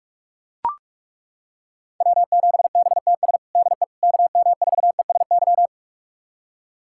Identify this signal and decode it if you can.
Morse code, heard as a keyed tone, about 41 wpm, 700 Hz. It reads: W7BTS DE KM4ESY
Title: USA Callsign DE Callsign 45wpm